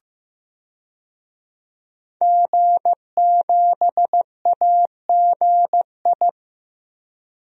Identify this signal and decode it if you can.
Morse code, heard as a keyed tone, about 15 wpm, 700 Hz. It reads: G7AGI
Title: DXCC-100 Call Signs - 5 Characters - Repeated 3x - 15wpm